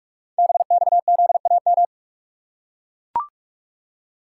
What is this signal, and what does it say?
Morse code, heard as a keyed tone, about 44 wpm, 700 Hz. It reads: BXZAM
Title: Code Groups - 5 Letters - 45wpm